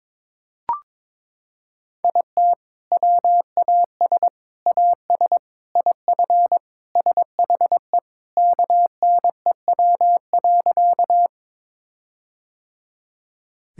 Morse code, heard as a keyed tone, about 22 wpm, 700 Hz. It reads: IT WAS AS IF SHE KNEW.
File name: practice / 22wpm / Sentences with 4-Word Phrases - 22wpm